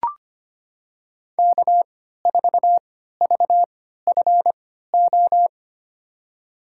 Morse code, heard as a keyed tone, about 25 wpm, spaced as for 14 wpm, 700 Hz. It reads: K4VFO
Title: USA Call Signs - 5 Characters - 3x Character Spacing - 25wpm